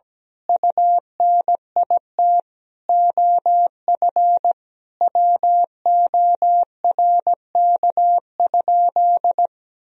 Morse code, heard as a keyed tone, about 17 wpm, 700 Hz. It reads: UNIT OF WORK?
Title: Sentences from Top 1000 Words - Review Part 2 - 17wpm